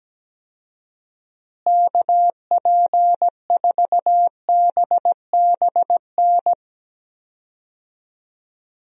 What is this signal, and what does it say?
Morse code, heard as a keyed tone, about 17 wpm, 700 Hz. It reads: KP4BBN
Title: DXCC-100 Call Signs - 6 Characters - Repeated 2x - 17wpm